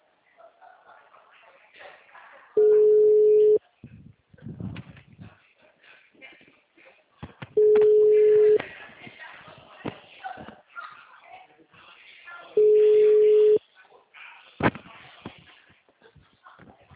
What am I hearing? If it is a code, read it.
Ringback tone (Europe, 425 Hz), with speech.